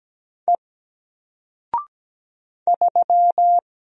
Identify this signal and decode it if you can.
Morse code, heard as a keyed tone, about 17 wpm, 700 Hz. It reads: E 3
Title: Single Character (Letter:Number:Punctuation:Prosign) Alt 17wpm